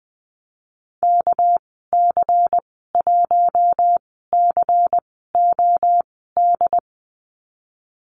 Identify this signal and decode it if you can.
Morse code, heard as a keyed tone, about 20 wpm, 700 Hz. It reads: K C 1 C O D
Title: 231 Lesson 37g - Letter x - Call Signs 20wpm